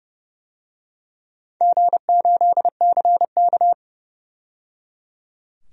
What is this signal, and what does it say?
Morse code, heard as a keyed tone, about 29 wpm, 700 Hz. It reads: G8CK
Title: DXCC-100 Call Signs - 4 Characters - 30wpm